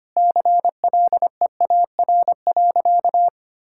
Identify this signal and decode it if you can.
Morse code, heard as a keyed tone, about 25 wpm, 700 Hz. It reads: CLEAR.
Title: Sentences from Top 1600 Words 25wpm